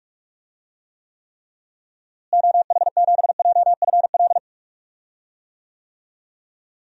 Morse code, heard as a keyed tone, about 45 wpm, 700 Hz. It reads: OH7JFL
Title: DXCC-100 Call Signs - 6 Characters - Repeated 3x - 45wpm